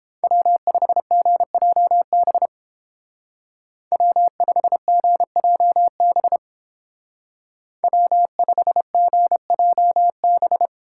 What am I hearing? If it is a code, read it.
Morse code, heard as a keyed tone, about 29 wpm, 700 Hz. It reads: W5GJB W5GJB W5GJB